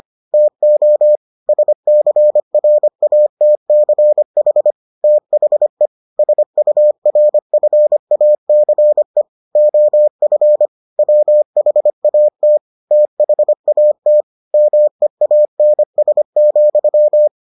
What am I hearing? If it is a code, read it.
Morse code, heard as a keyed tone, about 25 wpm, 600 Hz. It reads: TO SCRATCH THE SURFACE OF WHAT THAT MEANS,